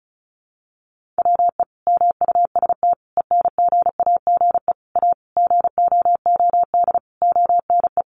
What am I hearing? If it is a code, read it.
Morse code, heard as a keyed tone, about 34 wpm, 700 Hz. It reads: WE MUST ENGAGE A GOOD ONE
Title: Sentences from Top 1500 Words 35wpm